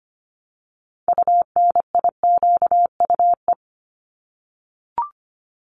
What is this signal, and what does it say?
Morse code, heard as a keyed tone, about 25 wpm, 700 Hz. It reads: UNIQUE
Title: Top 751-1000 Words 25wpm